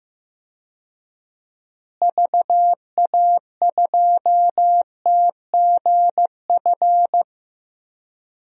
Morse code, heard as a keyed tone, about 15 wpm, 700 Hz. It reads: VA2TGF
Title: DXCC-100 Call Signs - 6 Characters - Repeated 3x - 15wpm